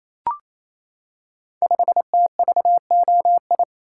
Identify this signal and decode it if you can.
Morse code, heard as a keyed tone, about 27 wpm, 700 Hz. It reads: 5TVOI
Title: Code Groups - 5 Letter:Numbers - 28wpm